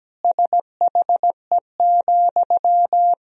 Morse code, heard as a keyed tone, about 17 wpm, 700 Hz. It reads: SHE,
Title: Sentences from Top 200 Words and Abbreviations 17wpm